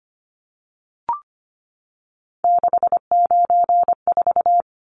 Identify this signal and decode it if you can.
Morse code, heard as a keyed tone, about 25 wpm, 700 Hz. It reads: 694
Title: Numbers - 3-Digit 25wpm